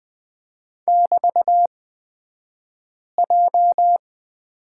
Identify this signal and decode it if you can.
Morse code, heard as a keyed tone, about 20 wpm, spaced as for 5 wpm, 700 Hz. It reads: <BT>J
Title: 223 Lesson 36d - BT - Two Characters 20wpm